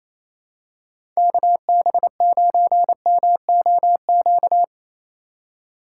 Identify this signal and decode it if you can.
Morse code, heard as a keyed tone, about 28 wpm, 700 Hz. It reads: KB9MOQ
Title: USA Call Signs - 6 Characters - 2x Character Spacing - 28wpm